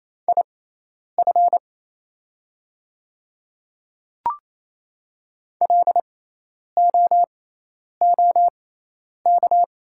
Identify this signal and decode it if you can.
Morse code, heard as a keyed tone, about 28 wpm, spaced as for 10 wpm, 700 Hz. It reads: IF LOOK
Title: 6x Character Spacing - Top 100 Words - 28wpm